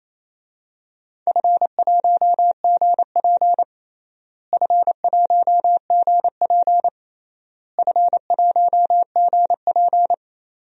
Morse code, heard as a keyed tone, about 28 wpm, 700 Hz. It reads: F1GP F1GP F1GP
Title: DXCC-100 Call Signs - 4 Characters - Repeated 3x - 28wpm